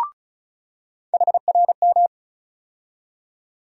Morse code, heard as a keyed tone, about 34 wpm, 700 Hz. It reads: HRM
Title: USA Call Sign Postfixes - 3 Letters 35wpm